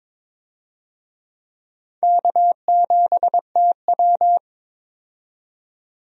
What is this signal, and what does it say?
Morse code, heard as a keyed tone, about 22 wpm, 700 Hz. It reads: K7TW
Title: USA Call Signs - 4 Characters - 22wpm